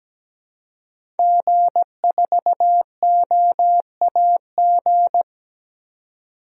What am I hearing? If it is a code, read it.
Morse code, heard as a keyed tone, about 17 wpm, 700 Hz. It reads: G4OAG
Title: DXCC-100 Call Signs - 5 Characters - 17wpm